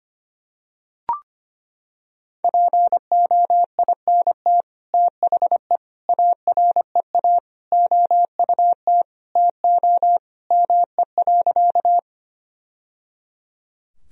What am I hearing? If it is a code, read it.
Morse code, heard as a keyed tone, about 25 wpm, 700 Hz. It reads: POINT THE AREA OUT TO ME.
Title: Sentences from Top 1000 Words - Review Part 2 - 25wpm